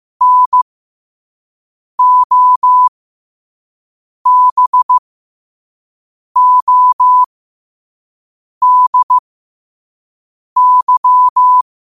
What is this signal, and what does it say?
Morse code, heard as a keyed tone, about 15 wpm, spaced as for 5 wpm, 1000 Hz. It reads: NOBODY
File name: qotdfiles / 5wpm